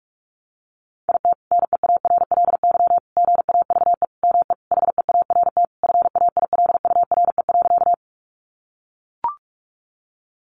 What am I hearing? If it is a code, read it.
Morse code, heard as a keyed tone, about 44 wpm, 700 Hz. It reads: IT NEARLY GAVE ME HEART FAILURE.
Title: Sentences from Top 1300 Words 45wpm